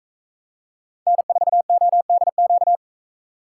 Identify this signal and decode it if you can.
Morse code, heard as a keyed tone, about 44 wpm, 700 Hz. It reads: N4ODQ